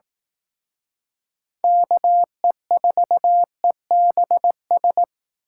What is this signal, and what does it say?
Morse code, heard as a keyed tone, about 18 wpm, 700 Hz. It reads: KE4EBS
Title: Speed Racing - USA Call Signs 15wpm